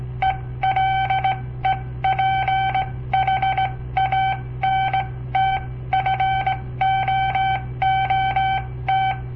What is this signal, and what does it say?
Morse code, heard as a keyed tone, about 17 wpm, spaced as for 15 wpm, 770 Hz. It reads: ELEPHANT FOOT
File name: papertapes / local / Code Practice-01